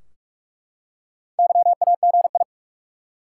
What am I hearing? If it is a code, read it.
Morse code, heard as a keyed tone, about 44 wpm, 700 Hz. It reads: YAGI
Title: Top 180 QSO Elements 45wpm